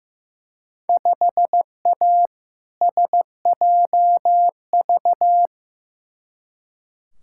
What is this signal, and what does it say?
Morse code, heard as a keyed tone, about 15 wpm, 700 Hz. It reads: 5A SJV
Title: Field Day Exchanges - 15wpm